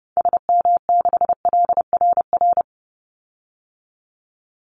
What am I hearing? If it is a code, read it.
Morse code, heard as a keyed tone, about 30 wpm, 700 Hz. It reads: SM6LRR
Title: Call Signs Repeated 3x - 30wpm